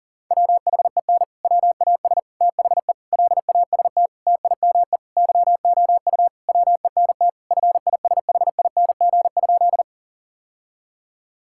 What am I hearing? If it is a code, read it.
Morse code, heard as a keyed tone, about 40 wpm, 700 Hz. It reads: WHEN WAS THE LAST TIME YOU WENT FISHING?
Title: Sentences from Top 1000 Words - Review Part 5 - 40wpm